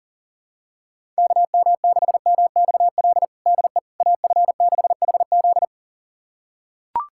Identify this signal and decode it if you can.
Morse code, heard as a keyed tone, about 40 wpm, 700 Hz. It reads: KM6MXL DE AF6HZ
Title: USA Callsign DE Callsign 40wpm